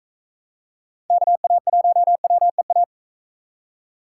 Morse code, heard as a keyed tone, about 42 wpm, 700 Hz. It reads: KA1WEA